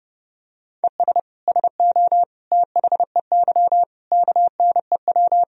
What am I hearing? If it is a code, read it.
Morse code, heard as a keyed tone, about 30 wpm, 700 Hz. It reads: ES SO THEY KNEW
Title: Sentences from Top 400 Words and Abbreviations 30wpm